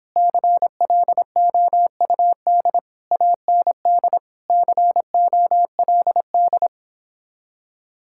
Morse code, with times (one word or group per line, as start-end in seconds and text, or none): 0.2-2.8 CLOUD
3.1-4.2 AND
4.5-6.7 COLD